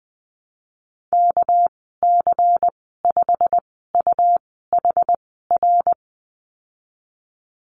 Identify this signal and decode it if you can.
Morse code, heard as a keyed tone, about 20 wpm, spaced as for 15 wpm, 700 Hz. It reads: KC5UHR
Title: 238 Lesson 38g - Letter q - Call Signs 20wpm